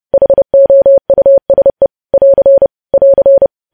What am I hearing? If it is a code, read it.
Morse code, heard as a keyed tone, about 30 wpm, 550 Hz. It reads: HOUSE <AR> <AR>
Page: news headlines in morse code at 30 wpm